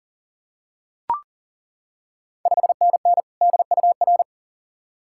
Morse code, heard as a keyed tone, about 39 wpm, 690 Hz. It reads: 5NN DUR
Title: Ontario QSO Party Exchange 40wpm